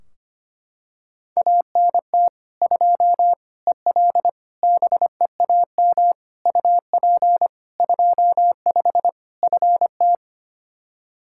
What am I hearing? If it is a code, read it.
Morse code, heard as a keyed tone, about 25 wpm, 700 Hz. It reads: ANT 2 EL BEAM UP 25 FT
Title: K1USN SST QSO Practice 25wpm